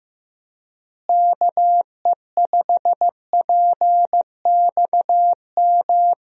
Morse code, heard as a keyed tone, about 15 wpm, 700 Hz. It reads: KE5PXM
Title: USA Call Signs - 6 Characters - Repeated 3x - 15wpm